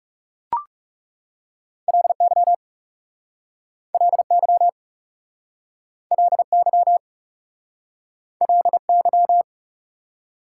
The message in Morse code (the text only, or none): LY LY LY LY